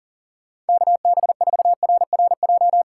Morse code, heard as a keyed tone, about 48 wpm, 700 Hz. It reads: KB4RRJ
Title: Speed Racing - USA 2x3 Call Signs 40wpm